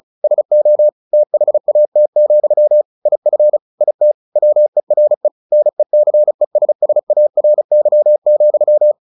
Morse code, heard as a keyed tone, about 34 wpm, 600 Hz. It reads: SO THAT, IF IT WERE NECESSARY,
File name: the old man and the sea / the old man and the sea 1 35wpm